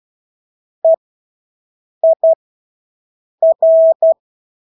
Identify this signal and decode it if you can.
Morse code, heard as a keyed tone, about 12 wpm, spaced as for 6 wpm, 650 Hz. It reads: EIR
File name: Engelske ord 12-6 wpm